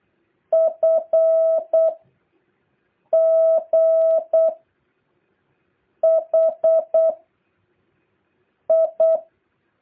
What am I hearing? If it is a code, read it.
Morse code, heard as a keyed tone, about 8 wpm, 660 Hz. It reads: F G H I
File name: jh8cbh / 2023cw